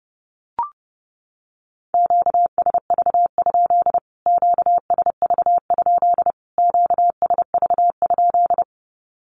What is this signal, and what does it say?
Morse code, heard as a keyed tone, about 30 wpm, 700 Hz. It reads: QSV? QSV? QSV?
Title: Q-Codes Repeated 3x 30wpm